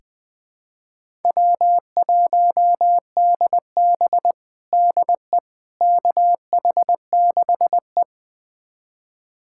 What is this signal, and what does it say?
Morse code, heard as a keyed tone, about 20 wpm, 700 Hz. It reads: W1DB DE KH6E
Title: USA Callsign DE Callsign - 4 Characters - 20wpm